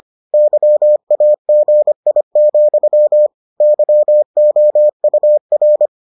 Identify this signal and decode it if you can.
Morse code, heard as a keyed tone, about 25 wpm, 600 Hz. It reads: YAGI, YOUR